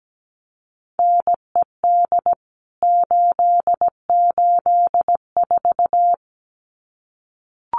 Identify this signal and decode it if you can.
Morse code, heard as a keyed tone, about 17 wpm, 700 Hz. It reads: NED 884
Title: CWT Exchange 17wpm